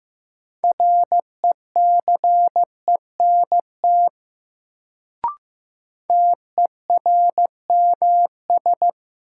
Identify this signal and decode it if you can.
Morse code, heard as a keyed tone, about 15 wpm, 700 Hz. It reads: RECENT TERMS
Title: Sets of 3 Words - New Word Drill - 15wpm